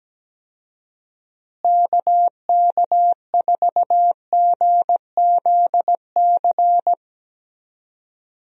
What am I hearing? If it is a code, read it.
Morse code, heard as a keyed tone, about 17 wpm, 700 Hz. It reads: KK4GZC
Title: USA Call Signs - 6 Characters - Repeated 2x - 17wpm